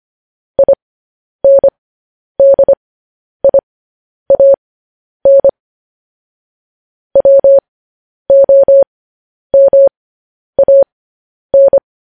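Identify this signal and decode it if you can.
Morse code, heard as a keyed tone, about 25 wpm, spaced as for 10 wpm, 550 Hz. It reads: INDIAN WOMAN
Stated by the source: Morse code news headlines broadcast.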